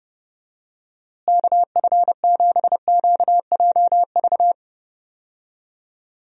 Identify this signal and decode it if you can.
Morse code, heard as a keyed tone, about 32 wpm, 700 Hz. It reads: KF7QJV
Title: USA Call Signs - 6 Characters - Repeated 3x - 30wpm